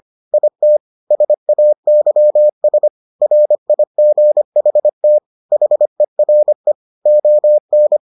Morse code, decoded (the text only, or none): IT SAYS RIGHT HERE ON